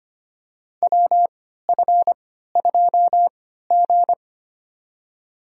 Morse code, heard as a keyed tone, about 25 wpm, spaced as for 14 wpm, 700 Hz. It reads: WF2G
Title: USA Call Signs - 4 Characters - 3x Character Spacing - 25wpm